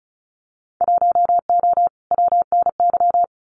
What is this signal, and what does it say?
Morse code, heard as a keyed tone, about 36 wpm, 690 Hz. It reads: 1O WNY
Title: Winter Field Day Exchanges 35wpm